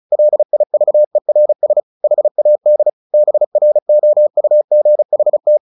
Morse code, heard as a keyed tone, about 34 wpm, 600 Hz. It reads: LIVERS HAD BROUGHT
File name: the old man and the sea / the old man and the sea 1 35wpm